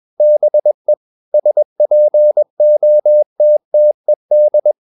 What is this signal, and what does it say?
Morse code, heard as a keyed tone, about 21 wpm, 600 Hz. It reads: BE SPOTTED
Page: CW med Gnister